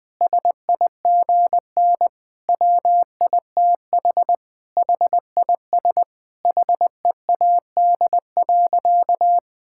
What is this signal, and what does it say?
Morse code, heard as a keyed tone, about 21 wpm, 700 Hz. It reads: SIGN WITH HIS HEAD.
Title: Sentences from Top 1000 Words - Review Part 6 - 20wpm